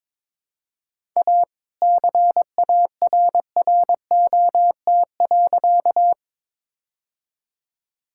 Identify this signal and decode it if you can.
Morse code, heard as a keyed tone, about 22 wpm, 700 Hz. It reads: A CARROT.
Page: morse-code-ninja-practice-22wpm